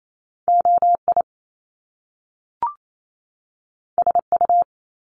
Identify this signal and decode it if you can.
Morse code, heard as a keyed tone, about 28 wpm, 700 Hz. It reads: OI SU
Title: Top Two Letter Combinations in Words 28wpm